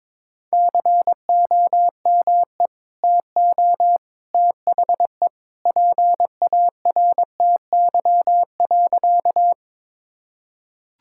Morse code, heard as a keyed tone, about 22 wpm, 700 Hz. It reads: COME TO THE PARTY.
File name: practice / 22wpm / Sentences with 2-Word Phrases - 22wpm